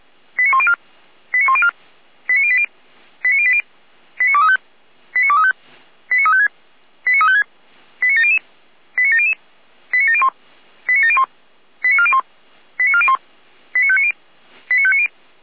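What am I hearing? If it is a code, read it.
Five-tone selective call (ZVEI), heard as quick runs of five tones, tones of about 80 ms. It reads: 89104 89104 89080 89080 89235 89235 89356 89356 89700 89700 89701 89701 89401 89401 89590 89590
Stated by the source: ZVEI five-tone call received by SDR.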